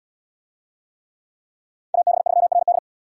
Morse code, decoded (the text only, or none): AB3RD